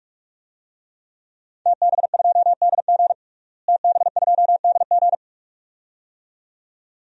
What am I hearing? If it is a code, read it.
Morse code, heard as a keyed tone, about 43 wpm, 700 Hz. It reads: TB2DG TB2DG
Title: DXCC-100 Call Signs - 5 Characters - Repeated 2x - 45wpm